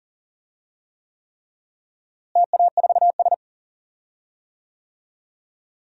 Morse code, heard as a keyed tone, about 40 wpm, 700 Hz. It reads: TA4S
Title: DXCC-100 Call Signs - 4 Characters - Repeated 3x - 40wpm